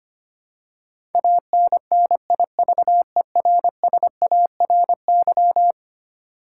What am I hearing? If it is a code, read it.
Morse code, heard as a keyed tone, about 25 wpm, 700 Hz. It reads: ANNIVERSARY
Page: morse-code-ninja-practice-25wpm